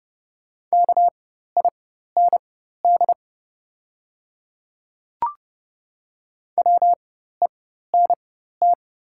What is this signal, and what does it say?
Morse code, heard as a keyed tone, about 32 wpm, spaced as for 14 wpm, 700 Hz. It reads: KIND WENT
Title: 4x Character Spacing - Top 200 Words - 30wpm